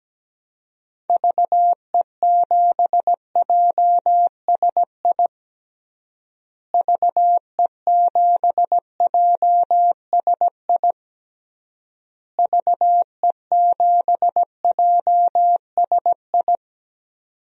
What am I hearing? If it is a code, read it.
Morse code, heard as a keyed tone, about 17 wpm, 700 Hz. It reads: VE7JSI VE7JSI VE7JSI
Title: DXCC-100 Call Signs - 6 Characters - Repeated 3x - 17wpm